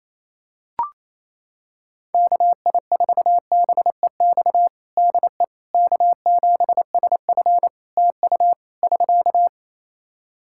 Morse code, heard as a keyed tone, about 28 wpm, 700 Hz. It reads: KI4BEX DE K7SF TU <SK>
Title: Practice QSO - Exchange 4 - Ending - 28wpm